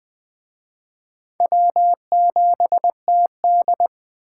Morse code, heard as a keyed tone, about 21 wpm, 700 Hz. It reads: W7TD
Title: Speed Racing - USA Call Signs 17wpm